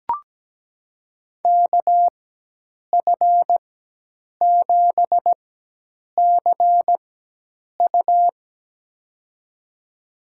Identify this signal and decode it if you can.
Morse code, heard as a keyed tone, about 17 wpm, spaced as for 8 wpm, 700 Hz. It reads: KF7CU
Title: USA Call Signs - 4x Character Spacing - 17wpm